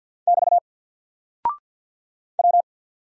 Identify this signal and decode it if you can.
Morse code, heard as a keyed tone, about 48 wpm, 700 Hz. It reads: <BT> W
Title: Single Character (Letter:Number:Punctuation:Prosign) Alt 50wpm